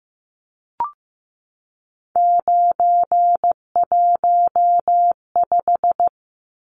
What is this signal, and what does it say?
Morse code, heard as a keyed tone, about 15 wpm, 700 Hz. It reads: 915
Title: Numbers - 3-Digit 15wpm